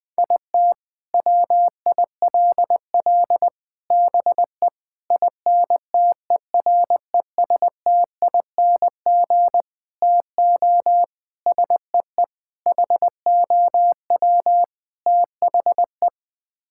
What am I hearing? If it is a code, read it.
Morse code, heard as a keyed tone, about 21 wpm, 700 Hz. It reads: IT WILL BE INTERESTING TO SEE HOW THE
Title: Sets of 8 Words - 20wpm